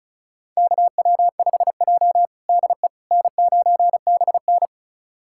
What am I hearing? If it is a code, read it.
Morse code, heard as a keyed tone, about 36 wpm, 700 Hz. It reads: KW5J DE N9BN
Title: USA Callsign DE Callsign - 4 Characters - 35wpm